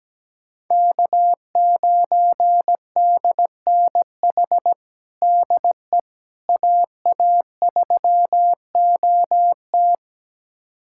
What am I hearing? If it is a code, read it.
Morse code, heard as a keyed tone, about 17 wpm, 700 Hz. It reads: K9DNH DE AA3OT
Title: USA Callsign DE Callsign - 5 Characters - 17wpm